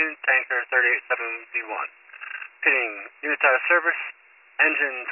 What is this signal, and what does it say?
Emergency Alert System (SAME) tones, heard as harsh buzzing bursts, with speech. It is heard from 0.5 to 4.0 s.